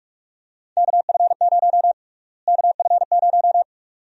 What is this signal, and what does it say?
Morse code, heard as a keyed tone, about 43 wpm, 700 Hz. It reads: KF0 KF0
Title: USA Call Sign Prefixes - Repeated 2x - 45wpm